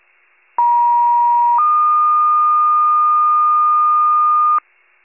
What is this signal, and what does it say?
Two-tone page, heard as a one-second tone, then a three-second tone, about 950 then 1220 Hz, with noise. It is heard from 0.6 to 4.6 s.
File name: pager / 41 RESCUE DUTY / wcalert